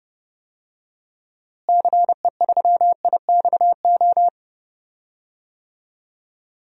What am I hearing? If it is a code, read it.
Morse code, heard as a keyed tone, about 30 wpm, 700 Hz. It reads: CE3IXO
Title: DXCC-100 Call Signs - 6 Characters - Repeated 2x - 30wpm